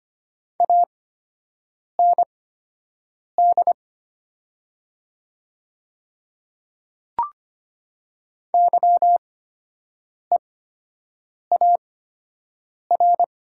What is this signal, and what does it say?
Morse code, heard as a keyed tone, about 25 wpm, spaced as for 7 wpm, 690 Hz. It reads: AND YEAR
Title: 8x Character Spacing - Top 300 Words - 25wpm